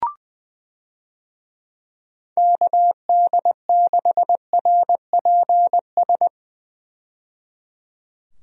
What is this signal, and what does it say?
Morse code, heard as a keyed tone, about 21 wpm, 700 Hz. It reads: KD6RPS
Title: USA Call Signs - 6 Characters - 20wpm